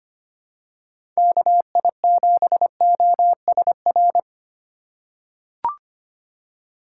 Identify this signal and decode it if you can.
Morse code, heard as a keyed tone, about 25 wpm, 690 Hz. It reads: KI7OSR
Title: USA Call Signs - 6 Characters - 2x Character Spacing - 25wpm